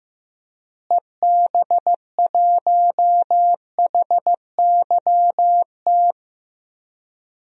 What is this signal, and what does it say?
Morse code, heard as a keyed tone, about 16 wpm, 700 Hz. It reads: EB1HYT
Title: DXCC-100 Call Signs - 6 Characters - Repeated 3x - 15wpm